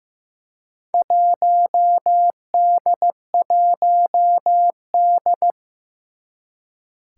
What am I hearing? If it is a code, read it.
Morse code, heard as a keyed tone, about 15 wpm, 700 Hz. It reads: 1D1D
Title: Four Challenging Letter-Number Combinations 15wpm